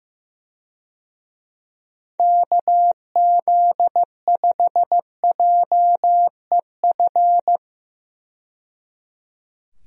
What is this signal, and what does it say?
Morse code, heard as a keyed tone, about 15 wpm, 700 Hz. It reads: KZ5JEF
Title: USA Call Signs - 6 Characters - 15wpm